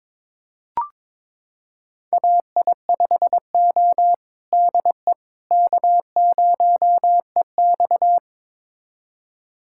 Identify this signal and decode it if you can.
Morse code, heard as a keyed tone, about 22 wpm, 700 Hz. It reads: AI5O DE K0EX
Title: USA Callsign DE Callsign - 4 Characters - 22wpm